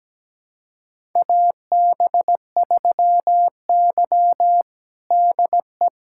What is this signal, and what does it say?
Morse code, heard as a keyed tone, about 17 wpm, 700 Hz. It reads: AB3Y DE
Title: USA Callsign DE Callsign - 4 Characters - 17wpm